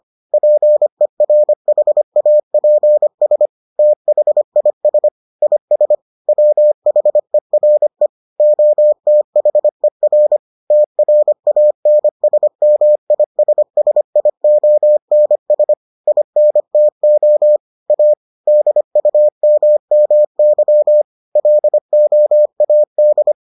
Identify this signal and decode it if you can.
Morse code, heard as a keyed tone, about 25 wpm, 600 Hz. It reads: PERHAPS THIS IS WHERE OTHER TRANSMISSIONS INTO A DUMMY LOAD